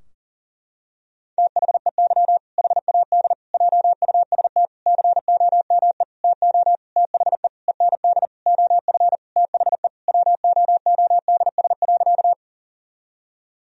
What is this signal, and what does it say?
Morse code, heard as a keyed tone, about 40 wpm, 700 Hz. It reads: THEY HAD JUST COME TO THE END OF THE WOODS.
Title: Sentences from Top 1000 Words - Review Part 2 - 40wpm